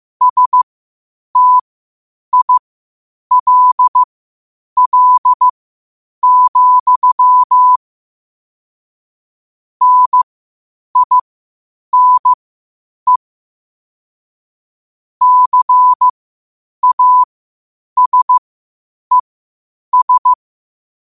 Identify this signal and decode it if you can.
Morse code, heard as a keyed tone, about 15 wpm, spaced as for 8 wpm, 1000 Hz. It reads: STILL, NINE CASES
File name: qotdfiles / 8wpm